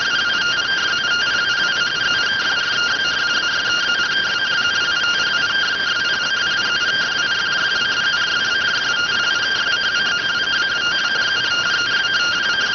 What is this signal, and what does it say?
Two-tone teleprinter signal, about 170 Hz shift.